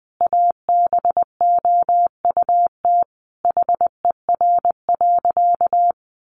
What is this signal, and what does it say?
Morse code, heard as a keyed tone, about 20 wpm, 700 Hz. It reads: ABOUT HER.
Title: Sentences from Top 800 Words 20wpm